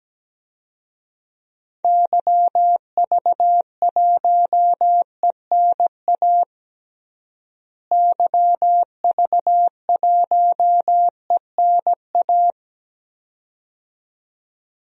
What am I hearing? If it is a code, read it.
Morse code, heard as a keyed tone, about 17 wpm, 700 Hz. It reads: YV1ENA YV1ENA
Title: DXCC-100 Call Signs - 6 Characters - Repeated 2x - 17wpm